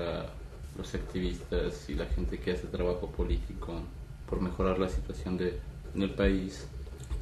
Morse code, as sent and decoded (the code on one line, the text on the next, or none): none